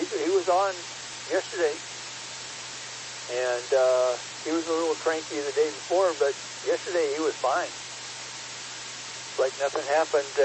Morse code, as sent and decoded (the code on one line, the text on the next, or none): none